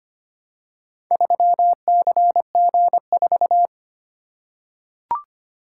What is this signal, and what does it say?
Morse code, heard as a keyed tone, about 25 wpm, 700 Hz. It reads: 3CG4
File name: practice / 25wpm / Four Letter-Number Combinations - 25wpm